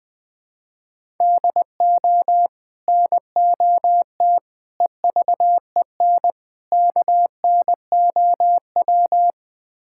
Morse code, heard as a keyed tone, about 20 wpm, 700 Hz. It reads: DO NOT EVEN KNOW
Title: Build A Phrase - 4-Word Phrases - 20wpm